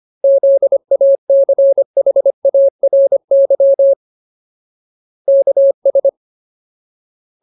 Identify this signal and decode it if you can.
Morse code, heard as a keyed tone, about 25 wpm, 550 Hz. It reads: ZACHARY KS
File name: Ultimate Word List / 25WPM Ultimate